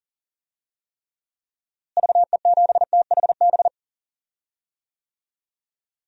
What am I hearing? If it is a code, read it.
Morse code, heard as a keyed tone, about 40 wpm, 700 Hz. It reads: VE7THB
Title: DXCC-100 Call Signs - 6 Characters - Repeated 2x - 40wpm